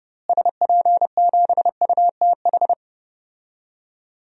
Morse code, heard as a keyed tone, about 30 wpm, 700 Hz. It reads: SP7UTH
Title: DXCC-100 Call Signs - 6 Characters - Repeated 2x - 30wpm